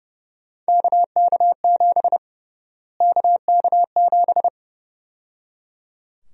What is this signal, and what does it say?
Morse code, heard as a keyed tone, about 30 wpm, 700 Hz. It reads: KK7 KK7
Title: USA Call Sign Prefixes - Repeated 2x - 30wpm